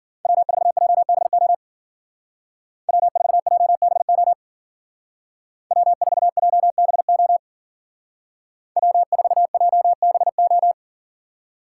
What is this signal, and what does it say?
Morse code, heard as a keyed tone, about 49 wpm, 700 Hz. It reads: W4JBO W4JBO W4JBO W4JBO
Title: Speed Racing - USA 1x3 Call Signs 40wpm